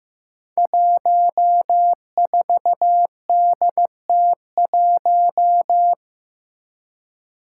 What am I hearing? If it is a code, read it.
Morse code, heard as a keyed tone, about 15 wpm, 700 Hz. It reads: 14DT1
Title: Five Challenging Letter-Number Combinations 15wpm